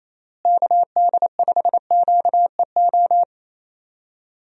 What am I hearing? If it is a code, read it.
Morse code, heard as a keyed tone, about 28 wpm, 700 Hz. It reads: KD5QEO
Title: USA Call Signs - 6 Characters - Repeated 3x - 28wpm